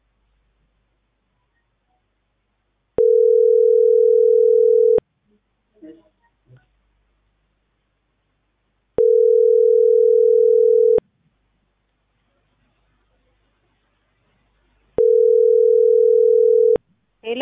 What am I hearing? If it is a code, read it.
Ringback tone (North America), 440 + 480 Hz.